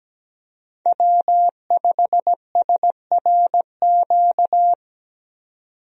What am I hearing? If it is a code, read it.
Morse code, heard as a keyed tone, about 17 wpm, 700 Hz. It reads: W5SRQ
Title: USA Call Signs - 5 Characters - 17wpm